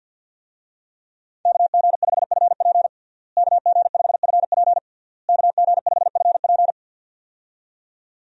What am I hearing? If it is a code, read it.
Morse code, heard as a keyed tone, about 52 wpm, 690 Hz. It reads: KG5FP KG5FP KG5FP
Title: USA Call Signs - 5 Characters - Repeated 3x - 50wpm